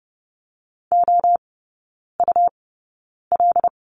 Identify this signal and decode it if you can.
Morse code, heard as a keyed tone, about 30 wpm, spaced as for 9 wpm, 700 Hz. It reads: OUL
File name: Lesson 17e - Letter u - Three Characters 30wpm